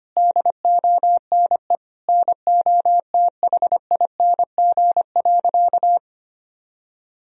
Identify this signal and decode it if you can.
Morse code, heard as a keyed tone, about 25 wpm, 700 Hz. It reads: DONE NOTHING.
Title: Sentences from Top 1000 Words - Review Part 3 - 25wpm